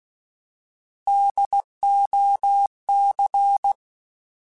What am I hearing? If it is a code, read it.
Morse code, heard as a keyed tone, about 16 wpm, 800 Hz. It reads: DOC
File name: Alphabetical / Alphabetical mp3s / Ds @15